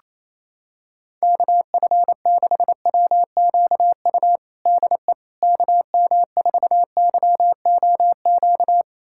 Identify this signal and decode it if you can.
Morse code, heard as a keyed tone, about 28 wpm, 700 Hz. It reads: KF6WQU DE KM4YOQ
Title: USA Callsign DE Callsign - 6 Characters - 28wpm